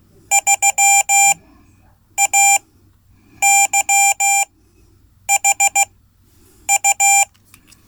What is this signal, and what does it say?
Morse code, heard as a keyed tone, about 15 wpm, spaced as for 8 wpm, 790 Hz. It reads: 3AYHU
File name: Code / 5min-c